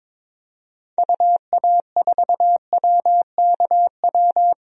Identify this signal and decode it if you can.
Morse code, heard as a keyed tone, about 21 wpm, 700 Hz. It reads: UA4WKW
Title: DXCC-100 Call Signs - 6 Characters - Repeated 3x - 22wpm